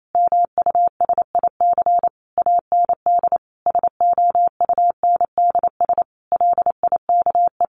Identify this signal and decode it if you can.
Morse code, heard as a keyed tone, about 28 wpm, 700 Hz. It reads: MUSIC AND SOUNDS LIKE
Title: A Sound Of Thunder 28wpm